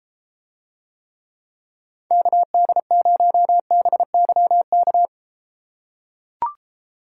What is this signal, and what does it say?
Morse code, heard as a keyed tone, about 33 wpm, 700 Hz. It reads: KD0BYK